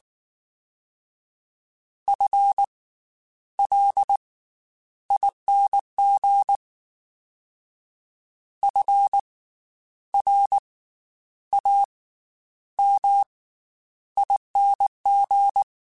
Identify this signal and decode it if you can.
Morse code, heard as a keyed tone, about 19 wpm, 800 Hz. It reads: F L ING F R A M ING
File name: Special INGs / Special INGs-mp3s / Slow ing @18